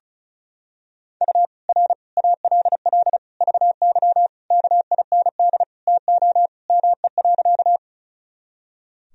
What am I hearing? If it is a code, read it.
Morse code, heard as a keyed tone, about 34 wpm, 700 Hz. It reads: U R ALL VY KIND TO ME.
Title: Sentences from Top 200 Words and Abbreviations 35wpm